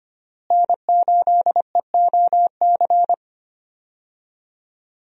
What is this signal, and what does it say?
Morse code, heard as a keyed tone, about 25 wpm, 700 Hz. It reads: N8EOC
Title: USA Call Signs - 5 Characters - Repeated 2x - 25wpm